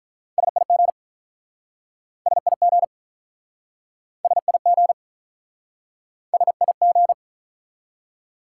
Morse code, heard as a keyed tone, about 48 wpm, 700 Hz. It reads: SIG SIG SIG SIG